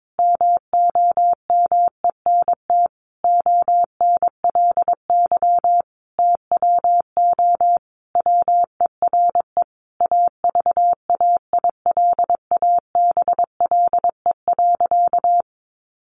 Morse code, heard as a keyed tone, about 22 wpm, 700 Hz. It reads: MOMENT ONLY TWO WERE AVAILABLE.
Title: Sentences from Top 1100 Words 22wpm